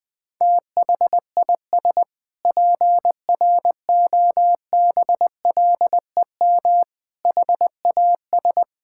Morse code, heard as a keyed tone, about 20 wpm, 700 Hz. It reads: THIS PROBLEM HAS
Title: Sentences with 8-Word Phrases - Spoken Beginning - 20wpm